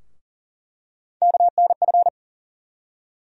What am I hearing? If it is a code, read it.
Morse code, heard as a keyed tone, about 40 wpm, 700 Hz. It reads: KNF
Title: USA Call Sign Postfixes - 3 Letters 40wpm